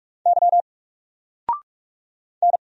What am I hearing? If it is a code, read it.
Morse code, heard as a keyed tone, about 44 wpm, 700 Hz. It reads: Y N
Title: Single Letter-Number Alt 45wpm